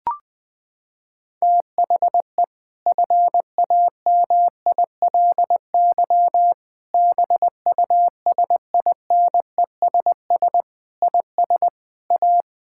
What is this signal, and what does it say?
Morse code, heard as a keyed tone, about 20 wpm, 700 Hz. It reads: THE FAMILY BUSINESS IS A
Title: Sentences from Top 1000 Words - Review Part 6 - 20wpm